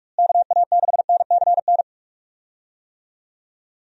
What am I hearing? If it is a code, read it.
Morse code, heard as a keyed tone, about 46 wpm, 700 Hz. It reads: KA6NCN
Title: USA Call Signs - 6 Characters - Repeated 2x - 45wpm